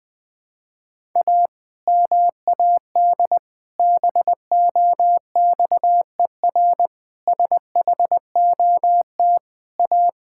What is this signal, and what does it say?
Morse code, heard as a keyed tone, about 20 wpm, 700 Hz. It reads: A MAD BOXER SHOT A
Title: Pangram Challenge v2 - 20wpm